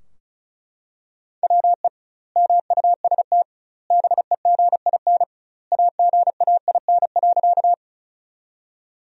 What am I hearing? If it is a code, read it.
Morse code, heard as a keyed tone, about 34 wpm, 700 Hz. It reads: WE MUST BEGIN AGAIN.
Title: 2x Word-Spacing - Sentences from Top 300 Words - 35wpm